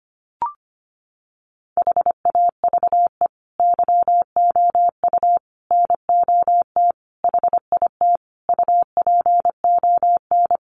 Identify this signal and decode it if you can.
Morse code, heard as a keyed tone, about 25 wpm, 700 Hz. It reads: HAVE YOU NOT HIT UPON
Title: Sentences from Top 1200 Words 25wpm